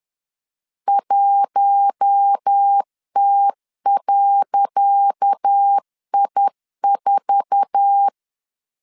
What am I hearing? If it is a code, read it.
Morse code, heard as a keyed tone, about 10 wpm, 800 Hz. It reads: 1T.I4
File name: LEZIONI / 10 SIGNS / 10C DOT 10wpm 2940